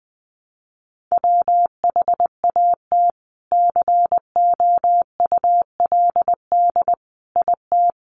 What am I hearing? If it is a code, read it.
Morse code, heard as a keyed tone, about 20 wpm, 700 Hz. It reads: WHAT COULD IT
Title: Sentences from Top 100 Words 20wpm